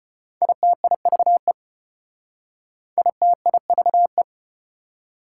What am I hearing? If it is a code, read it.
Morse code, heard as a keyed tone, about 33 wpm, 690 Hz. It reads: ITIVE ITIVE